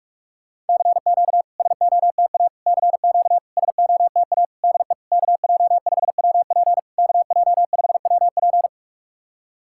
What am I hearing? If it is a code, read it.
Morse code, heard as a keyed tone, about 46 wpm, 660 Hz. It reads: CQ SOTA CQ SOTA DE KJ5WP KJ5WP
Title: SOTA QSO Practice 45wpm